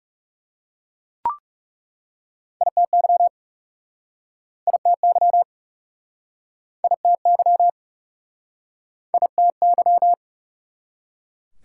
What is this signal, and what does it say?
Morse code, heard as a keyed tone, about 37 wpm, 690 Hz. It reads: ITY ITY ITY ITY
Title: Speed-Racing Prefixes and Suffixes 30wpm